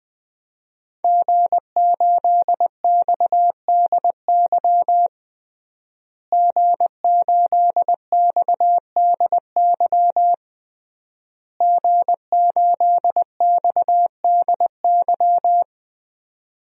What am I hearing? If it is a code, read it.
Morse code, heard as a keyed tone, about 20 wpm, 700 Hz. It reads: G8XDY G8XDY G8XDY
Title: DXCC-100 Call Signs - 5 Characters - Repeated 3x - 20wpm